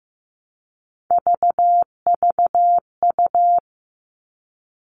Morse code, heard as a keyed tone, about 15 wpm, 700 Hz. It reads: VVU
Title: All Cut Numbers - 3-Digit 15wpm